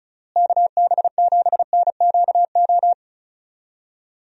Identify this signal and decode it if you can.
Morse code, heard as a keyed tone, about 36 wpm, 700 Hz. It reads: KB7NQO
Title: USA Call Signs - 6 Characters - Repeated 3x - 35wpm